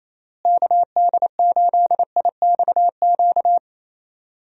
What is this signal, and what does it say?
Morse code, heard as a keyed tone, about 28 wpm, 700 Hz. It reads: KD8IXQ
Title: USA Call Signs - 6 Characters - Repeated 2x - 28wpm